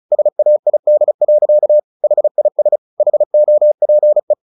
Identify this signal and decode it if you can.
Morse code, heard as a keyed tone, about 34 wpm, 600 Hz. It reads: SAID. HIS HOPE
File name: the old man and the sea / the old man and the sea 1 35wpm